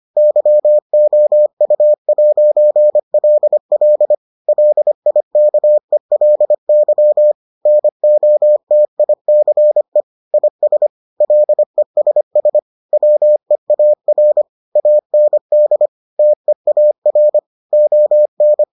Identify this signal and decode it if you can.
Morse code, heard as a keyed tone, about 25 wpm, 600 Hz. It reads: YOU'LL LIKELY NOTICE IS LESS WEAR AND TEAR ON